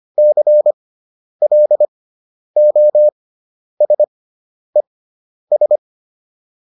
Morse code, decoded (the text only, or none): CLOSES